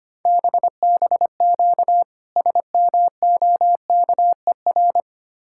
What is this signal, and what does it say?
Morse code, heard as a keyed tone, about 26 wpm, 700 Hz. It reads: BBQ SMOKER
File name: practice / 25wpm / Difficult Words - Speed-Racing with Hints - 25wpm